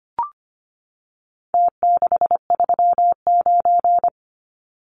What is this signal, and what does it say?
Morse code, heard as a keyed tone, about 25 wpm, 700 Hz. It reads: T639
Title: Numbers - Contest Style - 4-Digit 25wpm